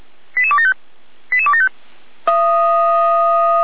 Five-tone selective call (ZVEI), heard as quick runs of five tones, tones of about 70 ms. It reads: 88275 88275